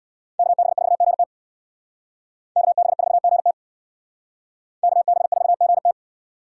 Morse code, heard as a keyed tone, about 62 wpm, 700 Hz. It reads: KB4GT KB4GT KB4GT
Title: Speed Racing - USA 2x2 Call Signs 50wpm